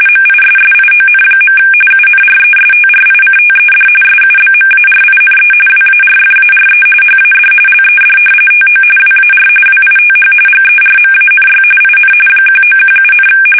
RTTY, about 45.45 baud. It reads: MISSION. UNRESTRICTED SUBMARINE WARFARE HAD LED TO WAR WITH GERMANY IN 1917,
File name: fsk850